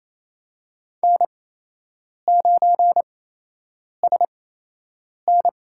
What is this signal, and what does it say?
Morse code, heard as a keyed tone, about 28 wpm, spaced as for 8 wpm, 700 Hz. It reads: N9SN
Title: USA Call Signs - 8x Character Spacing - 28wpm